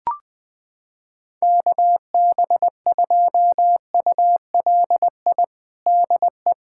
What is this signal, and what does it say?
Morse code, heard as a keyed tone, about 20 wpm, 700 Hz. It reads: KB2ULI DE
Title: USA Callsign DE Callsign 20wpm